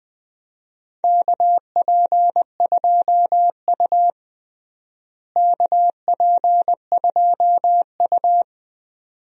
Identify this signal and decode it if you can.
Morse code, heard as a keyed tone, about 21 wpm, 700 Hz. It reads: KP2U KP2U
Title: DXCC-100 Call Signs - 4 Characters - Repeated 2x - 20wpm